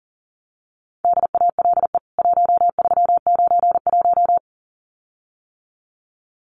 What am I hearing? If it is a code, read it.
Morse code, heard as a keyed tone, about 40 wpm, 700 Hz. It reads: DALE 1391
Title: CWT Exchange 40wpm